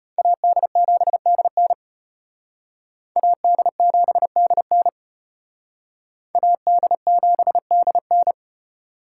Morse code, heard as a keyed tone, about 32 wpm, 700 Hz. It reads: AD7DN AD7DN AD7DN